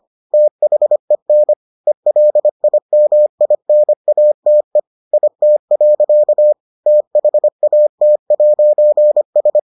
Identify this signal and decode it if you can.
Morse code, heard as a keyed tone, about 25 wpm, 600 Hz. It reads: THEN ELIMINATE IT. THAT'S